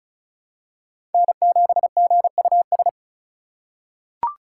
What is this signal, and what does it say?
Morse code, heard as a keyed tone, about 36 wpm, 700 Hz. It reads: N7GUS
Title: USA Call Signs - 5 Characters - 4x Character Spacing - 35wpm